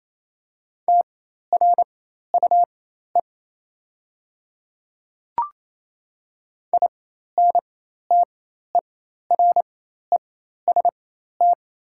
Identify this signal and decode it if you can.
Morse code, heard as a keyed tone, about 28 wpm, spaced as for 13 wpm, 700 Hz. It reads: TRUE INTEREST
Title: 4x Character Spacing - Top 500 Words - 28wpm